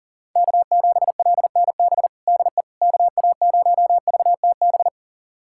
Morse code, heard as a keyed tone, about 37 wpm, 700 Hz. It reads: K7LNB DE KA0VTB
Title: USA Callsign DE Callsign 40wpm